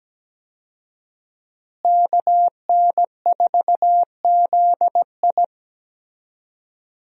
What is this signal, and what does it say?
Morse code, heard as a keyed tone, about 17 wpm, 700 Hz. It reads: KN4ZI
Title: USA Call Signs - 5 Characters - Repeated 2x - 17wpm